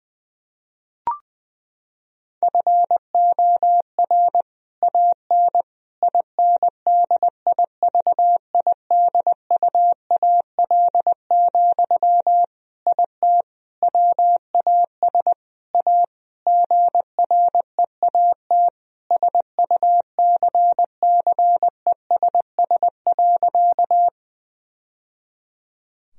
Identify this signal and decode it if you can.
Morse code, heard as a keyed tone, about 20 wpm, 700 Hz. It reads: FOR AN INDIVIDUAL, IT WAS A GREAT SUCCESS.
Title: Sentences from Top 1000 Words - Review Part 4 - 20wpm